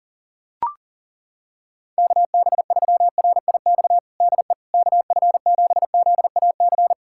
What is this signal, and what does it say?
Morse code, heard as a keyed tone, about 40 wpm, 700 Hz. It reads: KB3RIX DE KF7ZAC
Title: USA Callsign DE Callsign - 6 Characters - 40wpm